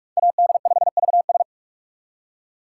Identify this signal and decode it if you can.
Morse code, heard as a keyed tone, about 43 wpm, 700 Hz. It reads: AD5VS
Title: USA Call Signs - 6x Character Spacing - 45wpm